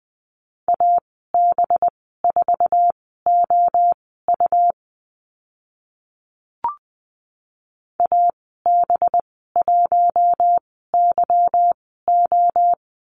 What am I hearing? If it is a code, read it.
Morse code, heard as a keyed tone, about 20 wpm, spaced as for 14 wpm, 700 Hz. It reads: AB4OU AB1YO
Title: 186 Lesson 30g - Letter b - Call Signs 20wpm